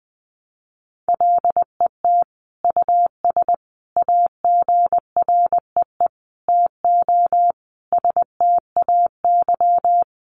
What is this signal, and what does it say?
Morse code, heard as a keyed tone, about 20 wpm, 700 Hz. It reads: LET US AGREE TO STAY
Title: Sentences from Top 1100 Words 20wpm